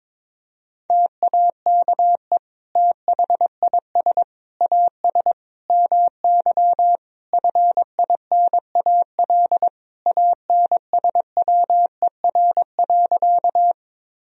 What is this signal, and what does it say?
Morse code, heard as a keyed tone, about 22 wpm, 700 Hz. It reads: TAKE THIS AS MY FINAL ANSWER.
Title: Sentences from Top 1000 Words - Review Part 2 - 22wpm